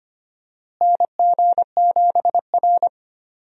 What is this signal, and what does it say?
Morse code, heard as a keyed tone, about 25 wpm, 700 Hz. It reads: NG7R
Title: USA Call Signs - 4 Characters - Repeated 2x - 25wpm